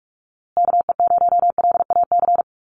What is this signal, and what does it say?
Morse code, heard as a keyed tone, about 44 wpm, 690 Hz. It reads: KE0LAC
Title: USA Call Signs Repeated 3x 45wpm